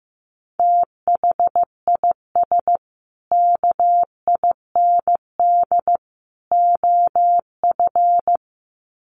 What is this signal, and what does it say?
Morse code, heard as a keyed tone, about 15 wpm, 700 Hz. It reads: THIS KIND OF
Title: Sets of 3 Words 15wpm